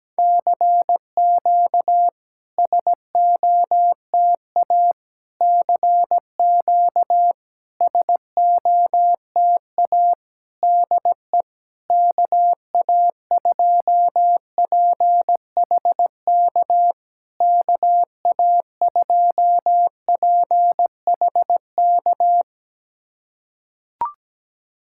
Morse code, heard as a keyed tone, about 17 wpm, 700 Hz. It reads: CQ SOTA CQ SOTA DE KA2PHK KA2PHK
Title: SOTA QSO Practice 17wpm